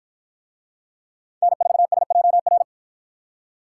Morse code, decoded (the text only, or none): N4SJR